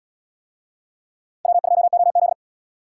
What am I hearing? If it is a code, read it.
Morse code, heard as a keyed tone, about 73 wpm, 700 Hz. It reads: W2OZ